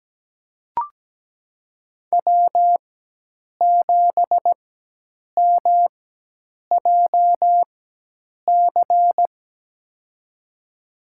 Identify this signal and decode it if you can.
Morse code, heard as a keyed tone, about 17 wpm, spaced as for 8 wpm, 700 Hz. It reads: W7MJC
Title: USA Call Signs - 5 Characters - 4x Character Spacing - 17wpm